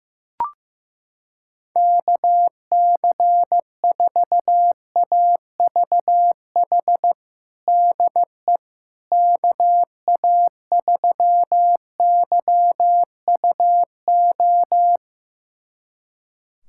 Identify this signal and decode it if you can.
Morse code, heard as a keyed tone, about 15 wpm, 700 Hz. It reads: KC4AVH DE KA3YUO
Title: USA Callsign DE Callsign - 6 Characters - 15wpm